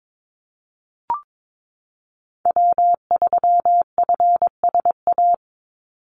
Morse code, heard as a keyed tone, about 22 wpm, 700 Hz. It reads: W3FSA
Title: Call Signs - 22wpm